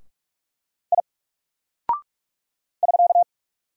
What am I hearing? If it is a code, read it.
Morse code, heard as a keyed tone, about 44 wpm, 700 Hz. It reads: I <SK>
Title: Single Character (Letter:Number:Punctuation:Prosign) Alt 45wpm